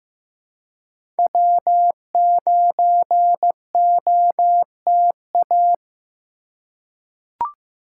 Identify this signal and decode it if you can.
Morse code, heard as a keyed tone, about 15 wpm, 700 Hz. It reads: W9OTA
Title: USA Call Signs - 5 Characters - 2x Character Spacing - 15wpm